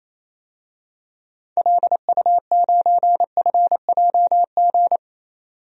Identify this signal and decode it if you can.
Morse code, heard as a keyed tone, about 28 wpm, 700 Hz. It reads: LU9FJG
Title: DXCC-100 Call Signs - 6 Characters - 28wpm